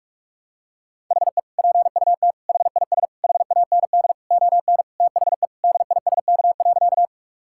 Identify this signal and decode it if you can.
Morse code, heard as a keyed tone, about 42 wpm, 700 Hz. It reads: HE PUT HIS HAND ON THE DISK.
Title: Sentences from Top 1000 Words - Review Part 6 - 45wpm